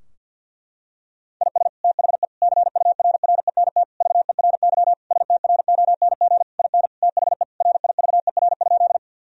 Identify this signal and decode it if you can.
Morse code, heard as a keyed tone, about 50 wpm, 700 Hz. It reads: IS THE CURRENT VERY STRONG IN THE RIVER?
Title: Sentences from Top 1000 Words - Review Part 4 - 50wpm